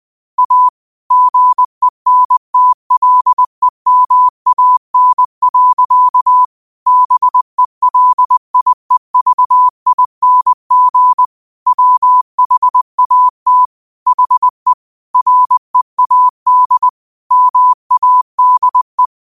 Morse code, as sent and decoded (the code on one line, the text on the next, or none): .- / --. -. - .-.. -- .- -. .-.-.- / -... .-.. .. ...- .. -. --. / .-- .... .- - / .... / .-. .- -.. / -- .- -.. .
A GENTLEMAN. BELIEVING WHAT HE READ MADE